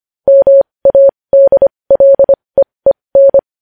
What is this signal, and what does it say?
Morse code, heard as a keyed tone, about 25 wpm, spaced as for 20 wpm, 550 Hz. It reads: MADLEEN
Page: News Headlines in Morse Code at 20 WPM